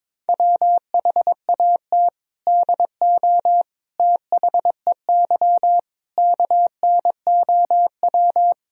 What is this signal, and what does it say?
Morse code, heard as a keyed tone, about 21 wpm, 700 Hz. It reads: WHAT DO THEY KNOW
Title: Sentences from Top 1000 Words - Review Part 3 - 22wpm